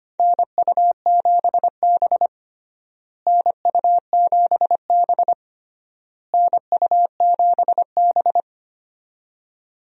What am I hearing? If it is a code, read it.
Morse code, heard as a keyed tone, about 25 wpm, 700 Hz. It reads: NU7B NU7B NU7B
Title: USA Call Signs - 4 Characters - Repeated 3x - 25wpm